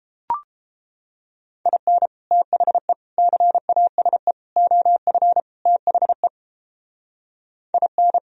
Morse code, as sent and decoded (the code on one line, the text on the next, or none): .. -. / - .... / -.-. .- ... / --- ..-. / - .... / .. -.
IN THE CASE OF THE IN